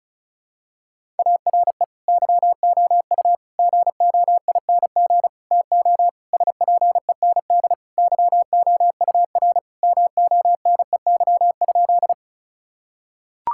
Morse code, heard as a keyed tone, about 34 wpm, 700 Hz. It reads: ARE YOU GOING TO SPEND YOUR MONEY?
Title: Sentences from Top 1000 Words - Review Part 5 - 35wpm